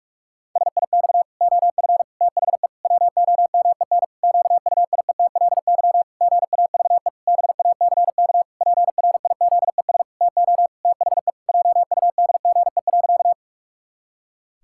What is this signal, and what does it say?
Morse code, heard as a keyed tone, about 46 wpm, 700 Hz. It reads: SIX OF THE WOMEN QUIETLY GAVE BACK PRIZES TO THE JUDGE.
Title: Pangram Challenge v2 - 45wpm